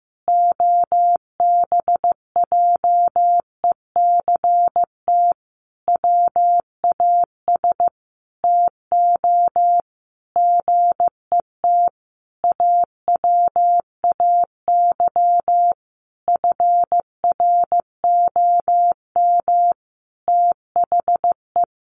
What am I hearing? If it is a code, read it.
Morse code, heard as a keyed tone, about 15 wpm, 700 Hz. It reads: OBJECT WAS TO GET AWAY FROM THE
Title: Sentences from Top 1500 Words 15wpm